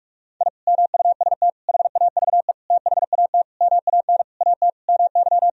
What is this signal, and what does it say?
Morse code, heard as a keyed tone, about 46 wpm, 700 Hz. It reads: I MUST HAVE THAT MAN AT MY
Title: Sentences from Top 1000 Words - Review Part 2 - 45wpm